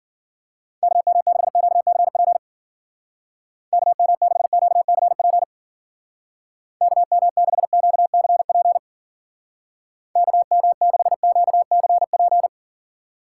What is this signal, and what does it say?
Morse code, heard as a keyed tone, about 51 wpm, 700 Hz. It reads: KM6QCP KM6QCP KM6QCP KM6QCP